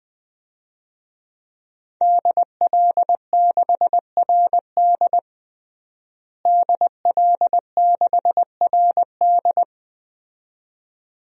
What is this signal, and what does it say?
Morse code, heard as a keyed tone, about 20 wpm, 700 Hz. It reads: DL6RD DL6RD
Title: DXCC-100 Call Signs - 5 Characters - Repeated 2x - 20wpm